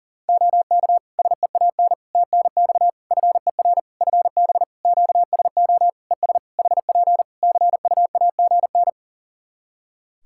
Morse code, heard as a keyed tone, about 40 wpm, 700 Hz. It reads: OK SEAN TNX FER FB QSO ES HP CUAGN
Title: Practice QSO - Exchange 4 - Ending - 40wpm